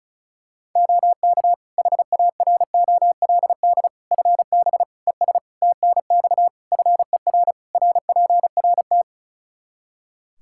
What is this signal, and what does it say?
Morse code, heard as a keyed tone, about 36 wpm, 700 Hz. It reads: OK HAROLD FB ES TNX FER RPRT
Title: Practice QSO - Exchange 2 - Info - 35wpm